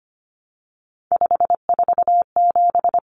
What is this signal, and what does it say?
Morse code, heard as a keyed tone, about 25 wpm, 700 Hz. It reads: 547
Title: Numbers - 3-Digit 25wpm